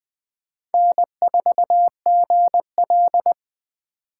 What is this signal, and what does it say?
Morse code, heard as a keyed tone, about 24 wpm, 700 Hz. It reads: N4GL
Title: Speed Racing - USA 1x2 Call Signs 20wpm